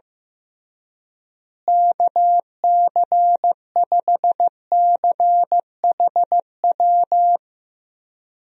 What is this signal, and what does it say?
Morse code, heard as a keyed tone, about 15 wpm, 700 Hz. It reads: KC5CHW
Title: USA Call Signs - 6 Characters - Repeated 3x - 15wpm